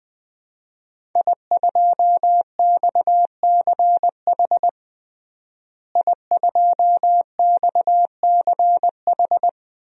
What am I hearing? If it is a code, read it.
Morse code, heard as a keyed tone, about 20 wpm, 700 Hz. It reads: I2XCH I2XCH
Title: DXCC-100 Call Signs - 5 Characters - Repeated 3x - 20wpm